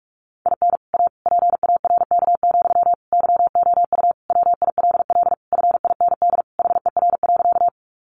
Morse code, heard as a keyed tone, about 44 wpm, 700 Hz. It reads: IN A PARK, YOU WILL FIND HER.
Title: Sentences from Top 1300 Words 45wpm